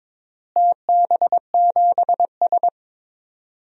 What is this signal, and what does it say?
Morse code, heard as a keyed tone, about 22 wpm, 700 Hz. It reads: TB7S
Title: DXCC-100 Call Signs - 4 Characters - Repeated 3x - 22wpm